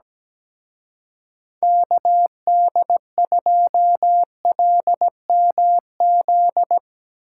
Morse code, heard as a keyed tone, about 18 wpm, 700 Hz. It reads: KD2LMZ
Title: Speed Racing - USA Call Signs 17wpm